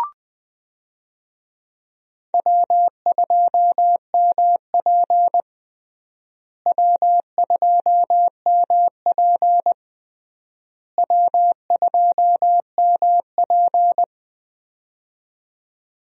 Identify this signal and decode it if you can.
Morse code, heard as a keyed tone, about 20 wpm, 700 Hz. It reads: W2MP W2MP W2MP
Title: USA Call Signs - 4 Characters - Repeated 3x - 20wpm